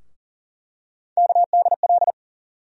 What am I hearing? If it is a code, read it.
Morse code, heard as a keyed tone, about 40 wpm, 700 Hz. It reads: KDL
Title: USA Call Sign Postfixes - 3 Letters 40wpm